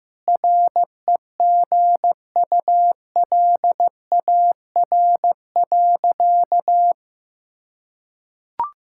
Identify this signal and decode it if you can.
Morse code, heard as a keyed tone, about 16 wpm, 700 Hz. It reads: REGULAR.
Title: Sentences from Top 1000 Words - Review Part 6 - 15wpm